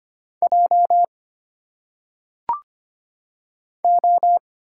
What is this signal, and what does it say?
Morse code, heard as a keyed tone, about 25 wpm, 700 Hz. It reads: J O
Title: Single Letters - Phonetic Pronunciation 25wpm